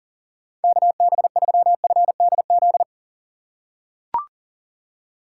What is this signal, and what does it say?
Morse code, heard as a keyed tone, about 40 wpm, 700 Hz. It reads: KB3FDZ
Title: USA Call Signs - 6 Characters - 4x Character Spacing - 40wpm